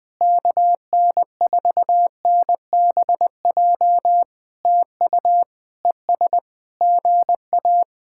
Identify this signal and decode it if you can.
Morse code, heard as a keyed tone, about 20 wpm, 700 Hz. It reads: KN4NBJ TU ES GA
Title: POTA QSO Practice - 20wpm